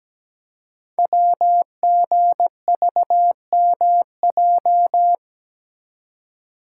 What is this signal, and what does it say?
Morse code, heard as a keyed tone, about 17 wpm, 700 Hz. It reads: WGVMJ
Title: Code Groups - 5 Letters - 17wpm